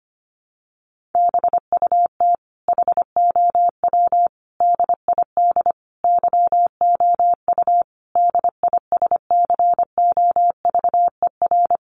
Morse code, heard as a keyed tone, about 25 wpm, 700 Hz. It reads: BUT HOW DID YOU DISCOVER
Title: Sentences from Top 1400 Words 25wpm